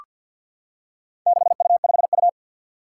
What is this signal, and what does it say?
Morse code, heard as a keyed tone, about 53 wpm, 700 Hz. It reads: 6U5U
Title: Four Challenging Letter-Number Combinations 50wpm